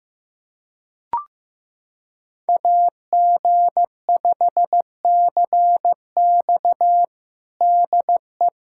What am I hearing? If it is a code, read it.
Morse code, heard as a keyed tone, about 15 wpm, 700 Hz. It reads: AG5CX DE
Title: USA Callsign DE Callsign - 5 Characters - 15wpm